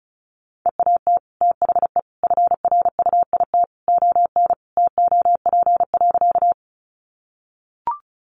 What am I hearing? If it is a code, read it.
Morse code, heard as a keyed tone, about 34 wpm, 700 Hz. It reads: EAT THE FRUIT ON TOP.
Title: Sentences from Top 1300 Words 35wpm